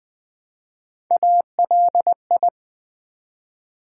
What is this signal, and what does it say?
Morse code, heard as a keyed tone, about 20 wpm, 700 Hz. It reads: ALI
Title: Build A Word - 1 Letter At A Time - 5-Letter Words - 20wpm